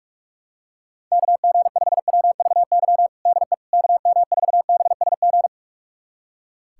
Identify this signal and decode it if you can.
Morse code, heard as a keyed tone, about 41 wpm, 700 Hz. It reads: KG5WVY DE KM4BSG
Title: USA Callsign DE Callsign - 6 Characters - 45wpm